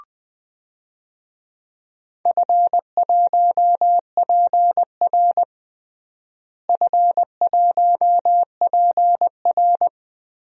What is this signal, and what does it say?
Morse code, heard as a keyed tone, about 21 wpm, 700 Hz. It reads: F1PR F1PR
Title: DXCC-100 Call Signs - 4 Characters - Repeated 3x - 20wpm